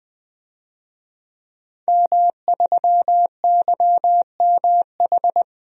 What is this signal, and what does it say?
Morse code, heard as a keyed tone, about 20 wpm, 700 Hz. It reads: M3YMH
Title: DXCC-100 Call Signs - 5 Characters - Repeated 2x - 20wpm